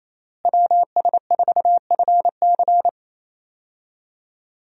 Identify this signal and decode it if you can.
Morse code, heard as a keyed tone, about 28 wpm, 700 Hz. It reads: WS4FC
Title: USA Call Signs - 5 Characters - Repeated 3x - 28wpm